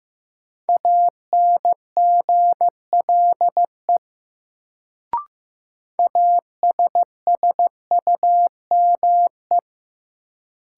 Morse code, heard as a keyed tone, about 15 wpm, 700 Hz. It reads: ANGLE ASSUME
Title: Sentences from Top 1500 Words - New Word Drill - 15wpm